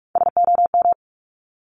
Morse code, heard as a keyed tone, about 46 wpm, 700 Hz. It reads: SOM
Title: Top Three Letter Combinations in Words 45wpm